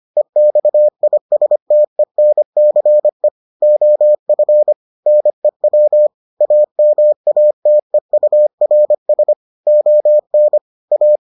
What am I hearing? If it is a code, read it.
Morse code, heard as a keyed tone, about 25 wpm, 600 Hz. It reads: EXISTENCE OF NEW AMATEURS ON A